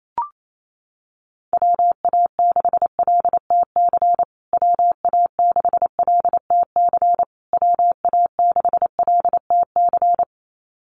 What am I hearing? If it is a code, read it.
Morse code, heard as a keyed tone, about 28 wpm, 700 Hz. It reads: WA6LTC WA6LTC WA6LTC
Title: USA Call Signs Repeated 3x 28wpm